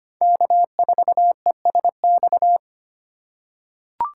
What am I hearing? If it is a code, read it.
Morse code, heard as a keyed tone, about 25 wpm, 690 Hz. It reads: K4ESX
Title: USA Call Signs - 5 Characters - 6x Character Spacing - 25wpm